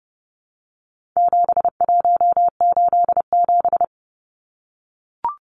Morse code, heard as a keyed tone, about 30 wpm, 700 Hz. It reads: 7187
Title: Numbers - 4-Digit 30wpm